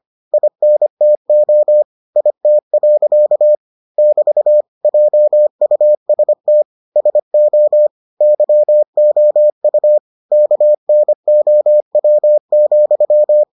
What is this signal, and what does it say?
Morse code, heard as a keyed tone, about 25 wpm, 600 Hz. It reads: INTO IT. <BT> JUST SO YOU KNOW,